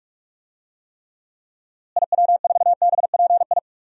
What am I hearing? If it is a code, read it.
Morse code, heard as a keyed tone, about 44 wpm, 700 Hz. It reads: IW4BPI
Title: DXCC-100 Call Signs - 6 Characters - Repeated 2x - 45wpm